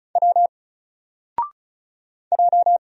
Morse code, heard as a keyed tone, about 36 wpm, 700 Hz. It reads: W J
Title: Single Letter-Number Alt 35wpm